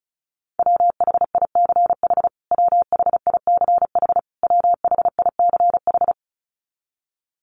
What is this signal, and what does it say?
Morse code, heard as a keyed tone, about 34 wpm, 700 Hz. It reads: WHICH WHICH WHICH
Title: Top 500 Words Repeated 35wpm